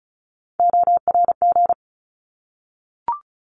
Morse code, heard as a keyed tone, about 34 wpm, 700 Hz. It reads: ORG
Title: ARRL Sections 35wpm